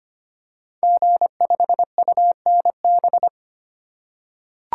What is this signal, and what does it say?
Morse code, heard as a keyed tone, about 25 wpm, 700 Hz. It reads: G5UNB
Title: Code Groups - 5 Letter:Numbers - 25wpm